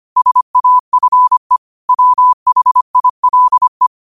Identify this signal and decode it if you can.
Morse code, heard as a keyed tone, about 25 wpm, 1000 Hz. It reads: SAFE WHILE